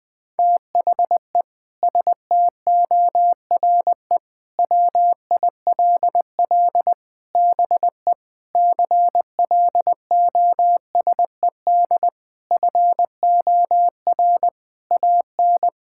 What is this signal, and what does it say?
Morse code, heard as a keyed tone, about 20 wpm, 700 Hz. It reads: THE STORE WILL BE CLOSED FOR AN
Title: Sentences with 6-Word Phrases - Spoken Beginning - 20wpm